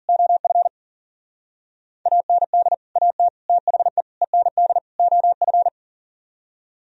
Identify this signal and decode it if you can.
Morse code, heard as a keyed tone, about 52 wpm, 700 Hz. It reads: OF AND AT THE END OF